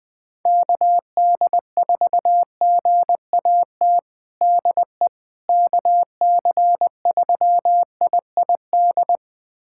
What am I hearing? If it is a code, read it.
Morse code, heard as a keyed tone, about 20 wpm, 700 Hz. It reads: KD4GAT DE KC3IID
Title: USA Callsign DE Callsign - 6 Characters - 20wpm